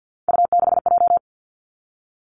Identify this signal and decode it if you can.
Morse code, heard as a keyed tone, about 50 wpm, 700 Hz. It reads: U6J
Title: Three Challenging Letter-Number Combinations 50wpm